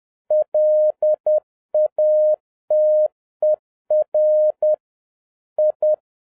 Morse code, heard as a keyed tone, about 10 wpm, 610 Hz. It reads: LATER I